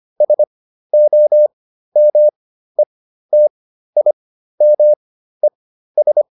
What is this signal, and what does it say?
Morse code, heard as a keyed tone, about 25 wpm, spaced as for 13 wpm, 600 Hz. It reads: SOMETIMES